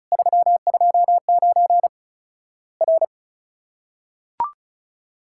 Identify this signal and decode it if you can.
Morse code, heard as a keyed tone, about 36 wpm, 700 Hz. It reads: 329 R
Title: Worked All Europe DX Contest - QTC Exchange - 35wpm